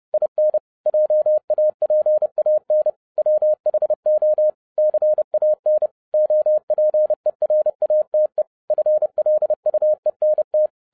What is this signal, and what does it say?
Morse code, heard as a keyed tone, about 30 wpm, 610 Hz. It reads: IN JAPAN WHO CAN OPERATE FLUENT
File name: 30wpm